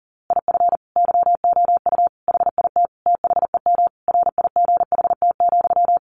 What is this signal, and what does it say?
Morse code, heard as a keyed tone, about 40 wpm, 700 Hz. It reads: IF YOU HIT THEM RIGHT,
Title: A Sound Of Thunder 40wpm